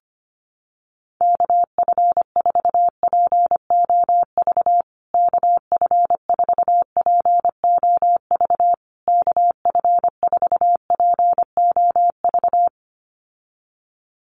Morse code, heard as a keyed tone, about 25 wpm, 700 Hz. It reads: KF4POV KF4POV KF4POV
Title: USA Call Signs Repeated 3x 25wpm